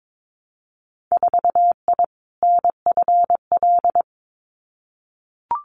Morse code, heard as a keyed tone, about 22 wpm, 700 Hz. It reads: 4I NFL
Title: Winter Field Day Exchanges 22wpm